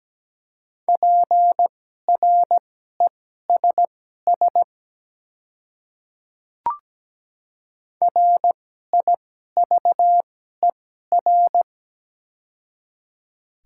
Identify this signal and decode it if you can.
Morse code, heard as a keyed tone, about 17 wpm, spaced as for 12 wpm, 700 Hz. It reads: PRESS RIVER
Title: 2x Character Spacing - Top 300 Words - 17wpm